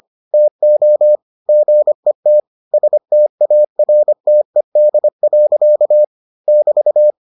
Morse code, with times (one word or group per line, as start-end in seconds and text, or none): 0.3-1.2 TO
1.5-2.4 GET
2.7-6.0 STARTED.
6.5-7.1 <BT>